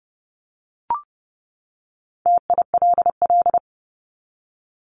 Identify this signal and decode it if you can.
Morse code, heard as a keyed tone, about 30 wpm, 700 Hz. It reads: TILL
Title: Top 1751-2000 Words 30wpm